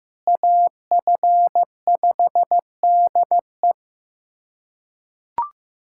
Morse code, heard as a keyed tone, about 15 wpm, 700 Hz. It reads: AF5DE
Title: USA Callsign DE Callsign - 5 Characters - 15wpm